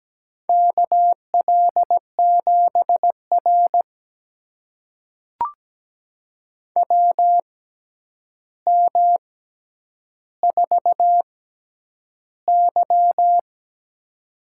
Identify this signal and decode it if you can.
Morse code, heard as a keyed tone, about 17 wpm, spaced as for 9 wpm, 700 Hz. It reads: KL7R WM4Y
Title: USA Call Signs - 4 Characters - 6x Character Spacing - 17wpm